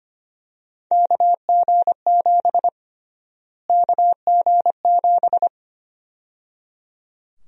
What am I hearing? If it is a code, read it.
Morse code, heard as a keyed tone, about 25 wpm, 700 Hz. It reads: KG7 KG7
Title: USA Call Sign Prefixes - Repeated 2x - 25wpm